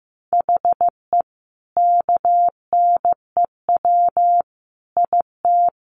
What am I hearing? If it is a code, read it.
Morse code, heard as a keyed tone, about 14 wpm, 700 Hz. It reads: HE KNEW IT
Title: Sentences from Top 1200 Words 15wpm